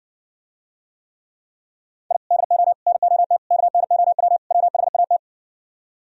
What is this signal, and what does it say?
Morse code, heard as a keyed tone, about 67 wpm, 700 Hz. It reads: I DO NOT KNOW WHAT